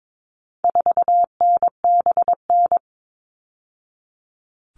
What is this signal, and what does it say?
Morse code, heard as a keyed tone, about 22 wpm, 700 Hz. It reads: 4NBN
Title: All Cut Numbers - 4-Digit 22wpm